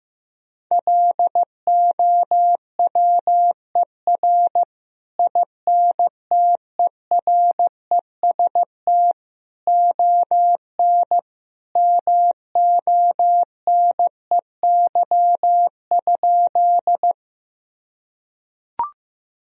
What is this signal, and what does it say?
Morse code, heard as a keyed tone, about 15 wpm, 700 Hz. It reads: LOWER INTEREST ON MONEY?
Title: Sentences from Top 1000 Words - Review Part 4 - 15wpm